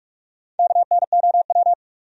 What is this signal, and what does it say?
Morse code, heard as a keyed tone, about 59 wpm, 700 Hz. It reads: KNOW